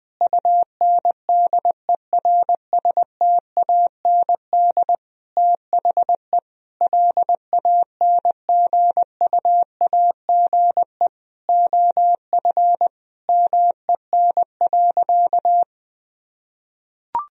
Morse code, heard as a keyed tone, about 21 wpm, 700 Hz. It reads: UNDERSTAND THE LANGUAGE OF MEN.
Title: Sentences from Top 1000 Words - Review Part 3 - 20wpm